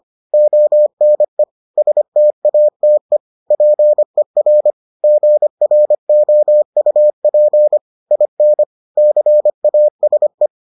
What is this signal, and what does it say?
Morse code, heard as a keyed tone, about 25 wpm, 600 Hz. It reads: ONE STATE PER GROUP IN CASE